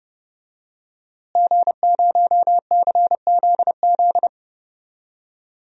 Morse code, heard as a keyed tone, about 31 wpm, 700 Hz. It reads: G0CZZ
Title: DXCC-100 Call Signs - 5 Characters - Repeated 3x - 30wpm